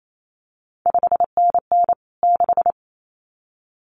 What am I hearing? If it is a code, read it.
Morse code, heard as a keyed tone, about 28 wpm, 700 Hz. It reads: 5NN 6
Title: CQ WW DX Contest Exchange - 28wpm